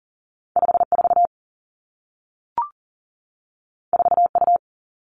Morse code, heard as a keyed tone, about 39 wpm, 700 Hz. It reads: 54 4U
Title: Two Challenging Letter-Number Combinations 40wpm